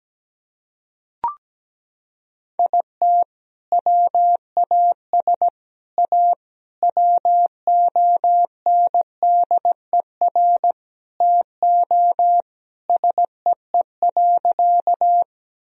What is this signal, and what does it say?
Morse code, heard as a keyed tone, about 17 wpm, 700 Hz. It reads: IT WAS A WONDER TO SEE.
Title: Sentences from Top 1000 Words - Review Part 3 - 17wpm